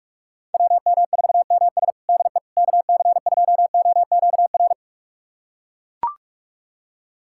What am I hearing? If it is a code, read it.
Morse code, heard as a keyed tone, about 41 wpm, 700 Hz. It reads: WM4MS DE KC2OQR
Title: USA Callsign DE Callsign 45wpm